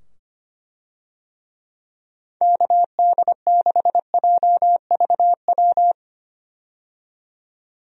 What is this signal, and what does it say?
Morse code, heard as a keyed tone, about 25 wpm, 700 Hz. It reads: KD6JVW
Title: USA Call Signs - 6 Characters - 25wpm